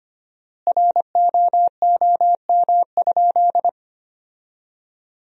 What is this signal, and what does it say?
Morse code, heard as a keyed tone, about 25 wpm, 700 Hz. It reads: ROOM?
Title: 4x Word-Spacing - Sentences from Top 300 Words - 25wpm